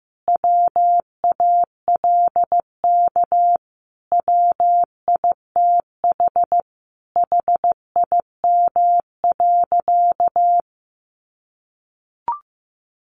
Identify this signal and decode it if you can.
Morse code, heard as a keyed tone, about 15 wpm, 700 Hz. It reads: WALK WITH HIM.
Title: Sentences from Top 1400 Words 15wpm